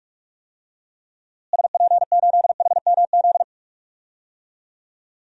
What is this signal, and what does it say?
Morse code, heard as a keyed tone, about 46 wpm, 700 Hz. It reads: SP8HMZ
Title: DXCC-100 Call Signs - 6 Characters - Repeated 3x - 45wpm